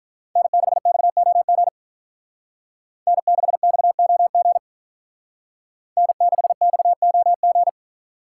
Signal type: Morse code

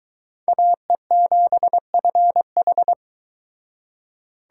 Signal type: Morse code

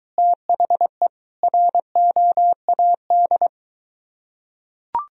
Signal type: Morse code

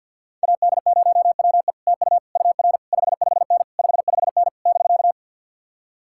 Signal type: Morse code

